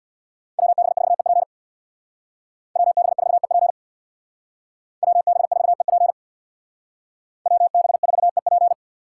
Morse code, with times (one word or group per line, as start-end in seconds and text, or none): 0.6-1.4 WB4EP
2.8-3.7 WB4EP
5.0-6.1 WB4EP
7.5-8.7 WB4EP